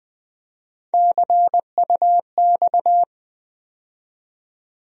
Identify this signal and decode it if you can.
Morse code, heard as a keyed tone, about 21 wpm, 700 Hz. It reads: CUX
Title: USA Call Sign Postfixes - 3 Letters 20wpm